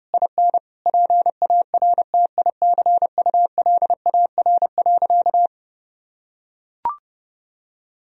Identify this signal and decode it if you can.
Morse code, heard as a keyed tone, about 32 wpm, 700 Hz. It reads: IN PARTICULAR.
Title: Sentences from Top 1000 Words - Review Part 4 - 30wpm